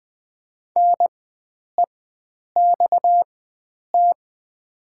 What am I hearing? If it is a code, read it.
Morse code, heard as a keyed tone, about 20 wpm, spaced as for 9 wpm, 700 Hz. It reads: NEXT
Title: 4x Character Spacing - Top 500 Words - 20wpm